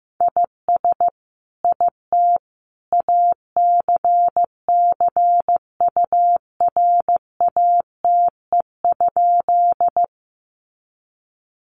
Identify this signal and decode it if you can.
Morse code, heard as a keyed tone, about 15 wpm, 700 Hz. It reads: IS IT ACCURATE?
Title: Sentences from Top 1200 Words 15wpm